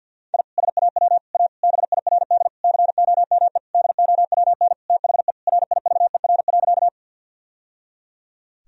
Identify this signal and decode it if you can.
Morse code, heard as a keyed tone, about 51 wpm, 700 Hz. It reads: I SAW A BIRD COME DOWN THE RIVER.
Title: Sentences from Top 1000 Words - Review Part 2 - 50wpm